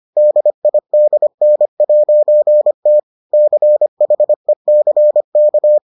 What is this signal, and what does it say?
Morse code, heard as a keyed tone, about 25 wpm, 600 Hz. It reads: DIDN'T CHECK